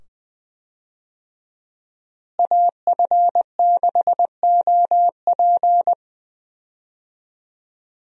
Morse code, heard as a keyed tone, about 20 wpm, 700 Hz. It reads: AF6OP
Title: USA Call Signs - 5 Characters - Repeated 2x - 20wpm